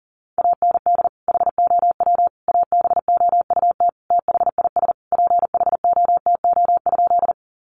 Morse code, heard as a keyed tone, about 40 wpm, 700 Hz. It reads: AND HOW ABOUT THIS PHOTO?
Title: Sentences from Top 1200 Words 40wpm